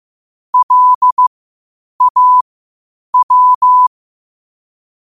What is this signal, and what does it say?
Morse code, heard as a keyed tone, about 15 wpm, spaced as for 8 wpm, 1000 Hz. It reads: LAW
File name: qotdfiles / 8wpm